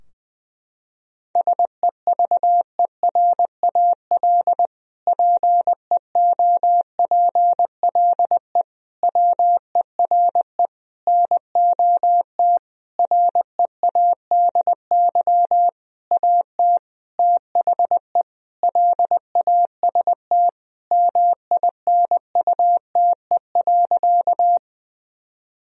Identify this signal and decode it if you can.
Morse code, heard as a keyed tone, about 21 wpm, 700 Hz. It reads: SEVERAL PEOPLE WERE NOT READY AT THE LAST MINUTE.
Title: Sentences from Top 1000 Words - Review Part 2 - 20wpm